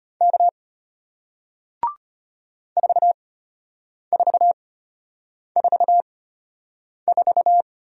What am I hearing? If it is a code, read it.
Morse code, heard as a keyed tone, about 33 wpm, spaced as for 8 wpm, 700 Hz. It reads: K 4444